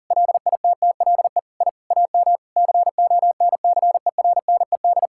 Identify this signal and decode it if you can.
Morse code, heard as a keyed tone, about 40 wpm, 700 Hz. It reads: LITTLE I AM CONCERNED
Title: Sentences from Top 1000 Words - Review Part 6 - 40wpm